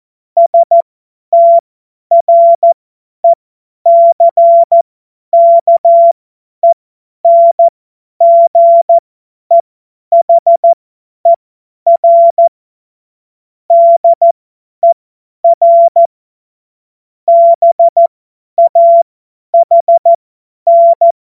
Morse code, heard as a keyed tone, about 14 wpm, spaced as for 10 wpm, 680 Hz. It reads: STRECKENGEHER DER BAHN
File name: Morsekurs / Witze / Witz 01. August 2023 50 F70